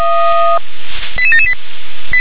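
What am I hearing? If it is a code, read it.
Five-tone selective call (ZVEI), heard as quick runs of five tones, tones of about 80 ms. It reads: 88778
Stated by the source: Five-tone ZVEI selective call received on an SDR.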